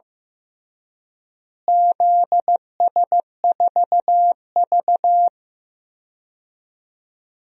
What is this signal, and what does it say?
Morse code, heard as a keyed tone, about 15 wpm, 700 Hz. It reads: ZS4V
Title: DXCC-100 Call Signs - 4 Characters - Repeated 3x - 15wpm